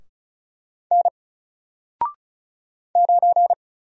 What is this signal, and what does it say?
Morse code, heard as a keyed tone, about 36 wpm, 700 Hz. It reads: N 9
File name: Single-Letter-Number-35wpm